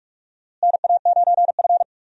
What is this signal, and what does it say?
Morse code, heard as a keyed tone, about 46 wpm, 700 Hz. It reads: NA9F
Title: USA Call Signs - 4 Characters - Repeated 2x - 45wpm